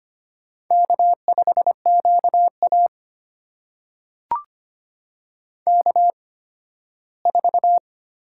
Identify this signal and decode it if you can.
Morse code, heard as a keyed tone, about 25 wpm, spaced as for 15 wpm, 700 Hz. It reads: K5QA K4
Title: USA Call Signs - 4 Characters - 8x Character Spacing - 25wpm